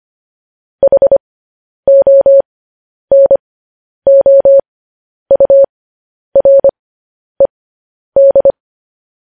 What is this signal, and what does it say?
Morse code, heard as a keyed tone, about 25 wpm, spaced as for 10 wpm, 550 Hz. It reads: HONOURED
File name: podcast / News headlines-1x-25x10